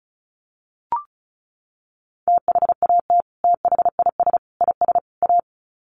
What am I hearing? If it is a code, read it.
Morse code, heard as a keyed tone, about 34 wpm, 700 Hz. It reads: THAT THIS IS A
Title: Sets of 4 Words 35wpm